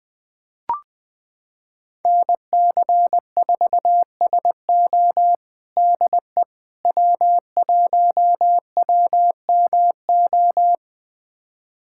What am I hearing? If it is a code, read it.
Morse code, heard as a keyed tone, about 20 wpm, 700 Hz. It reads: NC4SO DE W1WMO
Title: USA Callsign DE Callsign - 5 Characters - 20wpm